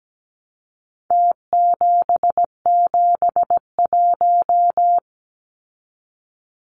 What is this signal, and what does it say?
Morse code, heard as a keyed tone, about 17 wpm, 700 Hz. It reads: T771
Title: Numbers - Contest Style - 4-Digit 17wpm